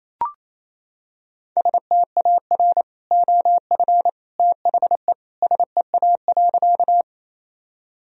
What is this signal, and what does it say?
Morse code, heard as a keyed tone, about 27 wpm, 700 Hz. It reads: STAR OF THE SEA.
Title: Sentences from Top 1000 Words - Review Part 2 - 28wpm